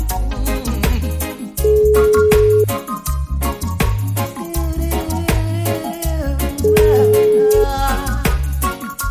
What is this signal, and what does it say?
Ringback tone (Europe, 425 Hz), with noise.